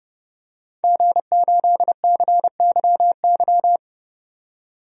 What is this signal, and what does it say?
Morse code, heard as a keyed tone, about 32 wpm, 700 Hz. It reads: G8CYY
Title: DXCC-100 Call Signs - 5 Characters - 30wpm